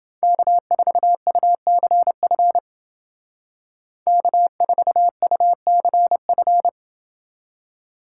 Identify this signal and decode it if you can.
Morse code, heard as a keyed tone, about 28 wpm, 700 Hz. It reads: K4UCF K4UCF